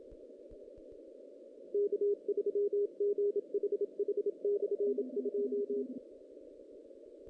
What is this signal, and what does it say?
Morse code, heard as a keyed tone, about 29 wpm, 440 Hz. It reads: K3GHH/2
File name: 2016 JAN CW NAQP